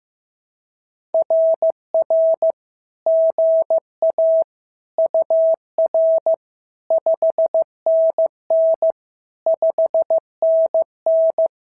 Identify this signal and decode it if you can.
Morse code, heard as a keyed tone, about 15 wpm, 650 Hz. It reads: RR GA UR 5NN 5NN
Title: SOTA QSO Practice 15wpm